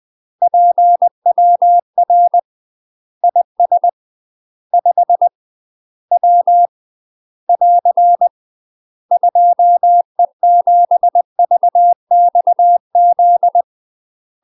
Morse code, heard as a keyed tone, about 20 wpm, 700 Hz. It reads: PWR IS 5 W <AR> 2E7VXZ
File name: UK MASTER QSOs TRAINING/20WPM